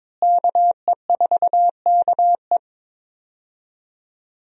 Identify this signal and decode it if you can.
Morse code, heard as a keyed tone, about 23 wpm, 700 Hz. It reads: KE4KE
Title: USA Call Signs - 5 Characters - Repeated 2x - 22wpm